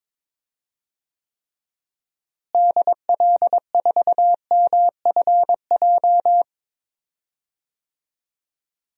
Morse code, heard as a keyed tone, about 21 wpm, 700 Hz. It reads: DL4MFJ
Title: DXCC-100 Call Signs - 6 Characters - Repeated 3x - 22wpm